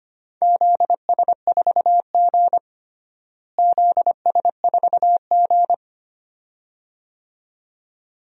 Morse code, heard as a keyed tone, about 25 wpm, 700 Hz. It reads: ZS4G ZS4G
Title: DXCC-100 Call Signs - 4 Characters - Repeated 2x - 25wpm